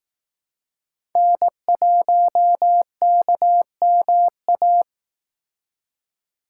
Morse code, heard as a keyed tone, about 18 wpm, 700 Hz. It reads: N1KMA
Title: Speed Racing - USA Call Signs 15wpm